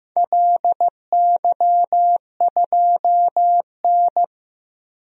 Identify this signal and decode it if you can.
Morse code, heard as a keyed tone, about 15 wpm, 700 Hz. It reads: LY2N
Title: DXCC-100 Call Signs - 4 Characters - Repeated 2x - 15wpm